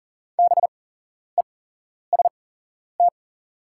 Morse code, heard as a keyed tone, about 40 wpm, spaced as for 11 wpm, 690 Hz. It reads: BEST
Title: 8x Character Spacing - Top 500 Words - 40wpm